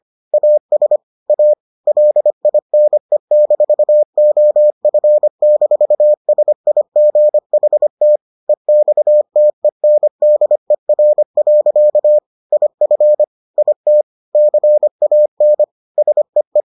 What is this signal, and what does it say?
Morse code, heard as a keyed tone, about 25 wpm, 600 Hz. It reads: AS A LINE-OF-SIGHT EXTENDER. IF IT CAN SEE